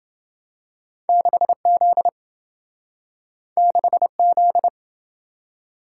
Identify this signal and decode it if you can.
Morse code, heard as a keyed tone, about 27 wpm, 700 Hz. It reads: 6Z 6Z